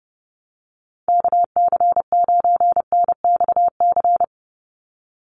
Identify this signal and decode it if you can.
Morse code, heard as a keyed tone, about 30 wpm, 700 Hz. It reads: KC9NXC
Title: USA Call Signs 30wpm